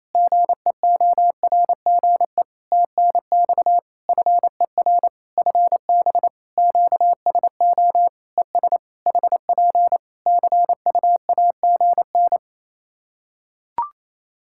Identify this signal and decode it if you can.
Morse code, heard as a keyed tone, about 28 wpm, 700 Hz. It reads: GEORGE TNX FER FB QSO ES HP CUAGN
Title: K1USN SST QSO Practice 28wpm